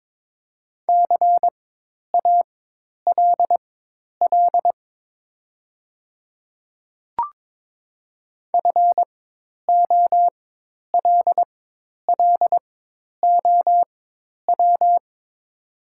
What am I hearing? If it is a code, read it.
Morse code, heard as a keyed tone, about 23 wpm, spaced as for 10 wpm, 690 Hz. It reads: CALL FOLLOW
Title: 4x Character Spacing - Top 300 Words - 22wpm